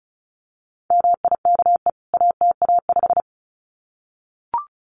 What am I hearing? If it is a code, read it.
Morse code, heard as a keyed tone, about 34 wpm, 700 Hz. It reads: MIKE ATA5
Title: CWT Exchange 35wpm